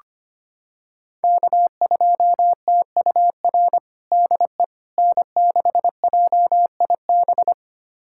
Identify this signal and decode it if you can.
Morse code, heard as a keyed tone, about 25 wpm, 700 Hz. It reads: K2TUR DE N6JIB
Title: USA Callsign DE Callsign - 5 Characters - 25wpm